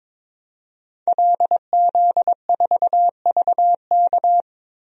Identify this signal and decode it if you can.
Morse code, heard as a keyed tone, about 22 wpm, 700 Hz. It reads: LZ4VK
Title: DXCC-100 Call Signs - 5 Characters - Repeated 3x - 22wpm